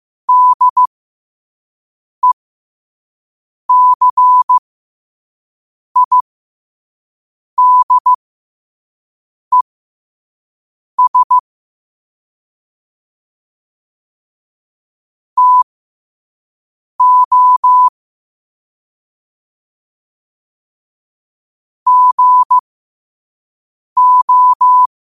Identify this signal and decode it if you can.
Morse code, heard as a keyed tone, about 15 wpm, spaced as for 5 wpm, 1000 Hz. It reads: DECIDES TO GO
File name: qotdfiles / 5wpm